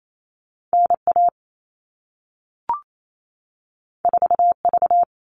Morse code, heard as a keyed tone, about 28 wpm, 700 Hz. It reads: NA 4V
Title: All Cut Numbers - 2-Digit 28wpm